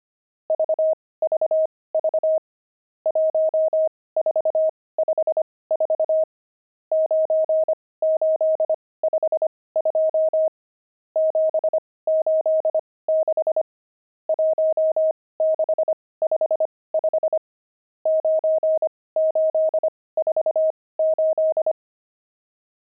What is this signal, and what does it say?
Morse code, heard as a keyed tone, about 25 wpm, spaced as for 18 wpm, 630 Hz. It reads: VVV 1454 9852 786 1655 9848
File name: pangram-fs-18-cs-25-nums17